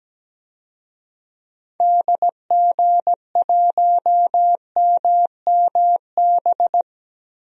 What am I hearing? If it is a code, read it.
Morse code, heard as a keyed tone, about 17 wpm, 700 Hz. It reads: DG1MMB
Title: DXCC-100 Call Signs - 6 Characters - Repeated 2x - 17wpm